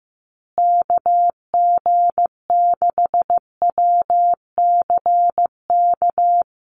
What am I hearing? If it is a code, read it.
Morse code, heard as a keyed tone, about 15 wpm, 700 Hz. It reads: KG6WCK
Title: USA Call Signs Repeated 3x 15wpm